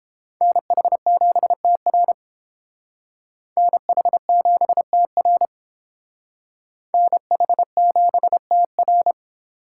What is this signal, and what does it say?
Morse code, heard as a keyed tone, about 29 wpm, 700 Hz. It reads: NH7TR NH7TR NH7TR